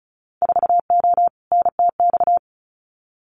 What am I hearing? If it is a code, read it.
Morse code, heard as a keyed tone, about 36 wpm, 700 Hz. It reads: 4O NTX
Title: Winter Field Day Exchanges 35wpm